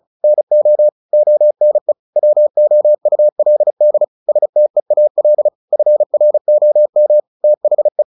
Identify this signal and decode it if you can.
Morse code, heard as a keyed tone, about 34 wpm, 600 Hz. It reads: NO ONE WOULD STEAL FROM THE
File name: the old man and the sea / the old man and the sea 1 35wpm